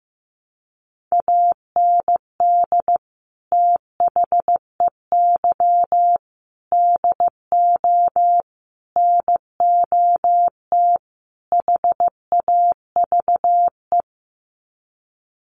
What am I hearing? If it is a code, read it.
Morse code, heard as a keyed tone, about 15 wpm, 700 Hz. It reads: AND THEY DO NOT HAVE
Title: Sets of 5 Words Encore2 15wpm